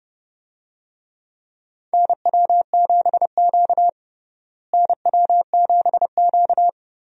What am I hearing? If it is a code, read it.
Morse code, heard as a keyed tone, about 30 wpm, 700 Hz. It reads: NW7Q NW7Q
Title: USA Call Signs - 4 Characters - Repeated 3x - 30wpm